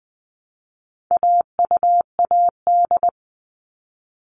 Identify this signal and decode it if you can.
Morse code, heard as a keyed tone, about 20 wpm, 700 Hz. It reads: AUAD
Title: All Cut Numbers - 4-Digit 20wpm